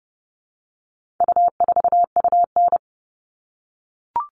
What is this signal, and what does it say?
Morse code, heard as a keyed tone, about 30 wpm, 700 Hz. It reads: U4UN
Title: All Cut Numbers - 4-Digit 30wpm